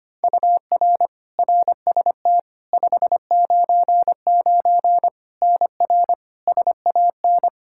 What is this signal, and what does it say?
Morse code, heard as a keyed tone, about 25 wpm, 700 Hz. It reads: UR RST 599 NR SAN
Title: Practice QSO - Exchange 1 - Beginning - 25wpm